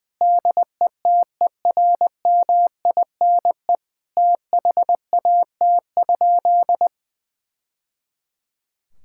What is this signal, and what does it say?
Morse code, heard as a keyed tone, about 20 wpm, 700 Hz. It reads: DETERMINE THAT?
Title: Sentences from Top 1000 Words - Review Part 5 - 20wpm